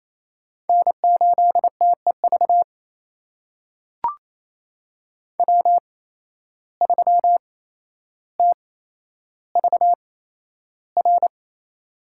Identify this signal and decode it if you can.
Morse code, heard as a keyed tone, about 28 wpm, spaced as for 12 wpm, 700 Hz. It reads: N8TEV W3TVR
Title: USA Call Signs - 5 Characters - 8x Character Spacing - 28wpm